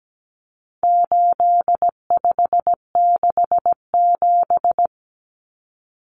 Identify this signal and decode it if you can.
Morse code, heard as a keyed tone, about 17 wpm, 700 Hz. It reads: 8567
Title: Numbers - 4-Digit 17wpm